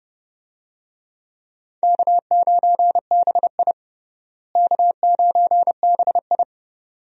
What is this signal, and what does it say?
Morse code, heard as a keyed tone, about 32 wpm, 700 Hz. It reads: K9BI K9BI
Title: USA Call Signs - 4 Characters - Repeated 2x - 30wpm